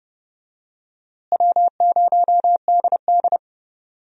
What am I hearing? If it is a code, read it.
Morse code, heard as a keyed tone, about 30 wpm, 700 Hz. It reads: W0DD
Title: USA Call Signs - 4 Characters - Repeated 2x - 30wpm